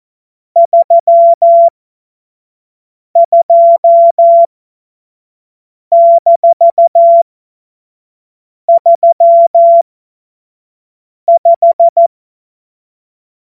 Morse code, heard as a keyed tone, about 14 wpm, spaced as for 5 wpm, 680 Hz. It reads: 32-35
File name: Morsekurs / Witze / Witz 04. Juli 2023 25 F70